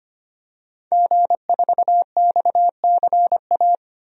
Morse code, heard as a keyed tone, about 25 wpm, 700 Hz. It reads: G4XCA
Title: DXCC-100 Call Signs - 5 Characters - 25wpm